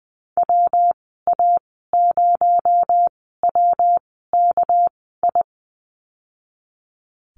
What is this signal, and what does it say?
Morse code, heard as a keyed tone, about 20 wpm, spaced as for 15 wpm, 700 Hz. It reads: WA0WKI
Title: 219 Lesson 35g - Number 0 - Call Signs 20wpm